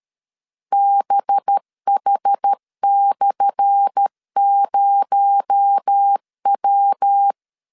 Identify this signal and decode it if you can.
Morse code, heard as a keyed tone, about 13 wpm, 800 Hz. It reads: BH/0W
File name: LEZIONI / 10 SIGNS / 10A SLASH 12wpm 2314